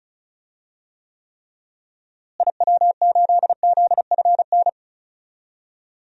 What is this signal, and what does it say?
Morse code, heard as a keyed tone, about 36 wpm, 700 Hz. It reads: IW8ZFN
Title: DXCC-100 Call Signs - 6 Characters - Repeated 3x - 35wpm